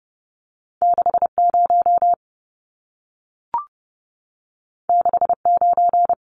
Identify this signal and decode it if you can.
Morse code, heard as a keyed tone, about 30 wpm, 700 Hz. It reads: 60 69
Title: Numbers - 2-Digit 30wpm